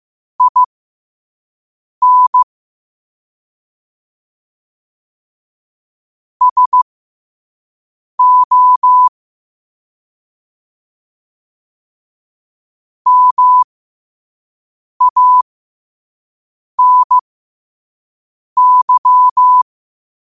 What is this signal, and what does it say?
Morse code, heard as a keyed tone, about 15 wpm, spaced as for 5 wpm, 1000 Hz. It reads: IN SO MANY